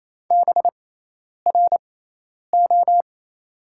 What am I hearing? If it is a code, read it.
Morse code, heard as a keyed tone, about 28 wpm, spaced as for 10 wpm, 700 Hz. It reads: BRO
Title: 6x Character Spacing - Top 500 Words - 28wpm